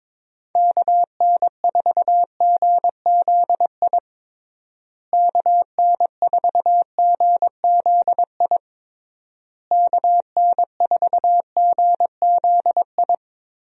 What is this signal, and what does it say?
Morse code, heard as a keyed tone, about 21 wpm, 700 Hz. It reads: KN4GZI KN4GZI KN4GZI
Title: USA Call Signs - 6 Characters - Repeated 3x - 22wpm